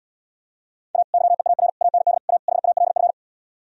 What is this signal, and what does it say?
Morse code, heard as a keyed tone, about 77 wpm, 700 Hz. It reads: A YEAR AND A HALF